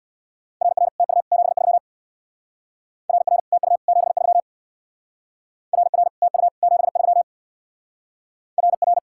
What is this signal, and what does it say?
Morse code, heard as a keyed tone, about 66 wpm, 700 Hz. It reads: RR TU 73 RR TU 73 RR TU 73 RR